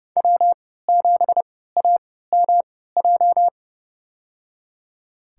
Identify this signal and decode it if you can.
Morse code, heard as a keyed tone, about 32 wpm, spaced as for 17 wpm, 700 Hz. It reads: W7AMJ
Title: USA Call Signs - 5 Characters - 3x Character Spacing - 30wpm